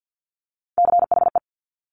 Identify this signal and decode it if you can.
Morse code, heard as a keyed tone, about 50 wpm, 700 Hz. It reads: CHE
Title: Top Three Letter Combinations in Words 50wpm